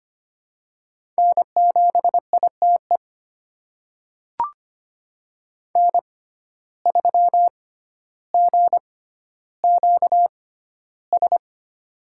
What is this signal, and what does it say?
Morse code, heard as a keyed tone, about 25 wpm, spaced as for 13 wpm, 700 Hz. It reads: N7ITE N3GQS
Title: USA Call Signs - 5 Characters - 6x Character Spacing - 25wpm